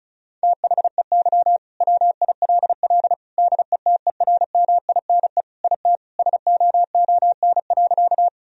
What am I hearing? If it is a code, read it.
Morse code, heard as a keyed tone, about 36 wpm, 700 Hz. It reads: THEY WILL DETERMINE IT SOON.
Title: Sentences from Top 1000 Words - Review Part 4 - 35wpm